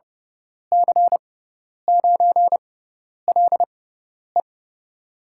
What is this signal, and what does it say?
Morse code, heard as a keyed tone, about 30 wpm, spaced as for 10 wpm, 700 Hz. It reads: C9LE
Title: USA Call Signs - 5 Characters - 6x Character Spacing - 30wpm